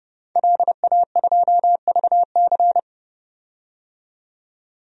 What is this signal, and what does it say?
Morse code, heard as a keyed tone, about 30 wpm, 700 Hz. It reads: LA2VC
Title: DXCC-100 Call Signs - 5 Characters - 30wpm